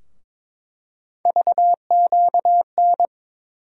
Morse code, heard as a keyed tone, about 21 wpm, 700 Hz. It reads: VQN
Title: USA Call Sign Postfixes - 3 Letters 22wpm